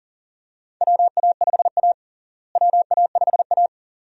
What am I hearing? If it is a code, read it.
Morse code, heard as a keyed tone, about 40 wpm, 700 Hz. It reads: WA5A WA5A
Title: USA Call Signs - 4 Characters - Repeated 2x - 40wpm